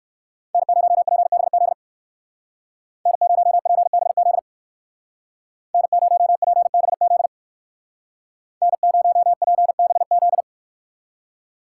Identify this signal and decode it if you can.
Morse code, heard as a keyed tone, about 65 wpm, 700 Hz. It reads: N0PBZ N0PBZ N0PBZ N0PBZ